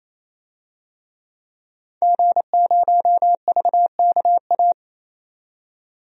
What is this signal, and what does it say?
Morse code, heard as a keyed tone, about 28 wpm, 700 Hz. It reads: G0VKA
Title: DXCC-100 Call Signs - 5 Characters - 28wpm